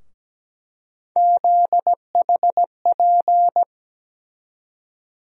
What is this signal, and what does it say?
Morse code, heard as a keyed tone, about 17 wpm, 700 Hz. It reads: ZHP
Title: USA Call Sign Postfixes - 3 Letters 17wpm